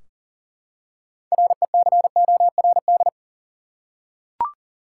Morse code, heard as a keyed tone, about 40 wpm, 700 Hz. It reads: RECORD
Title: Six Letter Words - 40wpm